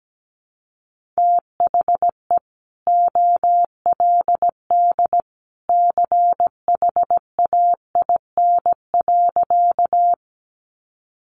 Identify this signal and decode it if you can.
Morse code, heard as a keyed tone, about 17 wpm, 700 Hz. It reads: THE OLD CHAIN.
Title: Sentences from Top 1500 Words 17wpm